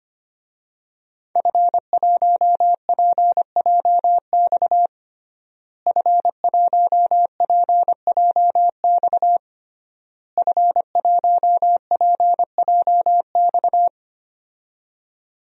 Morse code, heard as a keyed tone, about 25 wpm, 700 Hz. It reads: F1PJX F1PJX F1PJX
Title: DXCC-100 Call Signs - 5 Characters - Repeated 3x - 25wpm